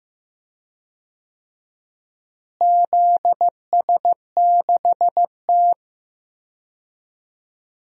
Morse code, heard as a keyed tone, about 15 wpm, 700 Hz. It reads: ZS6T
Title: DXCC-100 Call Signs - 4 Characters - Repeated 3x - 15wpm